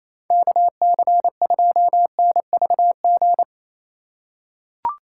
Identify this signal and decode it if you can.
Morse code, heard as a keyed tone, about 28 wpm, 700 Hz. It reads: KC2NVG
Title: USA Call Signs - 4x Character Spacing - 28wpm